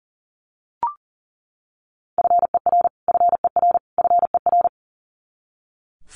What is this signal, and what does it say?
Morse code, heard as a keyed tone, about 40 wpm, 700 Hz. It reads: FER FER FER
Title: Abbreviations Repeated 3x - 40wpm